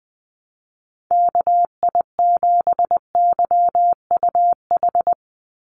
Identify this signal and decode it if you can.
Morse code, heard as a keyed tone, about 20 wpm, 700 Hz. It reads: KI7YUH
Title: USA Call Signs 20wpm